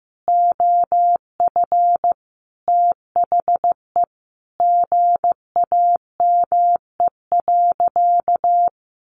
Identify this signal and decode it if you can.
Morse code, heard as a keyed tone, about 15 wpm, 700 Hz. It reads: OF THE GAME.
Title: Sentences from Top 1400 Words 15wpm